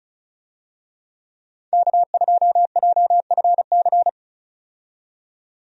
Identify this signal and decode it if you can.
Morse code, heard as a keyed tone, about 34 wpm, 700 Hz. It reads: K2JFC
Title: USA Call Signs - 5 Characters - 35wpm